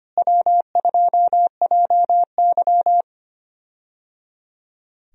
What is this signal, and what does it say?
Morse code, heard as a keyed tone, about 25 wpm, 700 Hz. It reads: W2JY
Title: USA Call Signs - 4 Characters - Repeated 3x - 25wpm